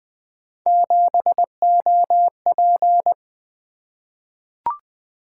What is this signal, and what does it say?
Morse code, heard as a keyed tone, about 20 wpm, 700 Hz. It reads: 7OP
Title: Three Letter-Number Combinations - 20wpm